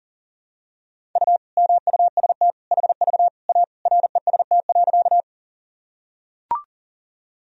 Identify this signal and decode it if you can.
Morse code, heard as a keyed tone, about 40 wpm, 700 Hz. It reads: U MUST HV A REST.
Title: Sentences from Top 500 Words and Abbreviations 40wpm